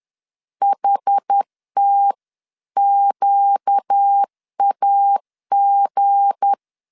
Morse code, heard as a keyed tone, about 10 wpm, 800 Hz. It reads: HTQAG
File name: LEZIONI / 7 GF9Z / 7 GF9 10wpm 2747